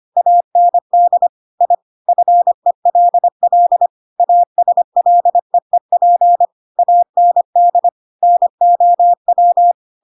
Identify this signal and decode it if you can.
Morse code, heard as a keyed tone, about 25 wpm, 700 Hz. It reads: AND I FELL ASLEEP AND NOW
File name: Pinocchio / Pinocchio ch7 25 25 ik1hge